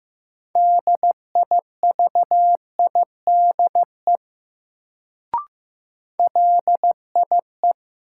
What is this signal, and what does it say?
Morse code, heard as a keyed tone, about 15 wpm, 700 Hz. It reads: DIVIDE LIE
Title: Sentences from Top 1600 Words - New Word Drill Alt - 15wpm